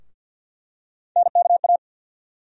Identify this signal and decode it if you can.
Morse code, heard as a keyed tone, about 52 wpm, 700 Hz. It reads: NKA
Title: USA Call Sign Postfixes - 3 Letters 50wpm